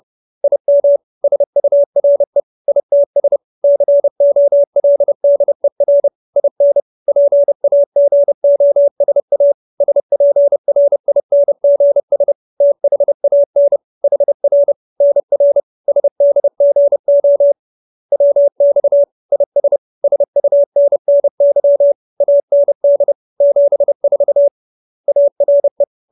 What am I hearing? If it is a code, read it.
Morse code, heard as a keyed tone, about 30 wpm, 570 Hz. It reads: IM SURE ITS COLDER IN PAGOSA SPRINGS THAN HR NR SDGO WX IS SUNNY AND 74 ARE